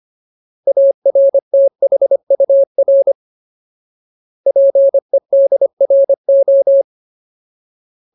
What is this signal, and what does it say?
Morse code, heard as a keyed tone, about 25 wpm, 550 Hz. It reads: ARTHUR PEDRO
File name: Ultimate Word List / 25WPM Ultimate